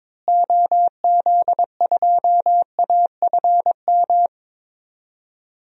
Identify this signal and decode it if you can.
Morse code, heard as a keyed tone, about 22 wpm, 700 Hz. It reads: OZ2AFM
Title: DXCC-100 Call Signs - 6 Characters - Repeated 2x - 22wpm